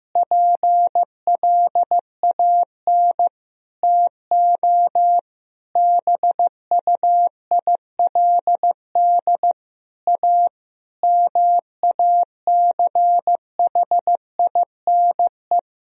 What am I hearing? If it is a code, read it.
Morse code, heard as a keyed tone, about 15 wpm, 700 Hz. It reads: PLAN TO BUILD A MACHINE
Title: Sentences from Top 1000 Words - Review Part 2 - 15wpm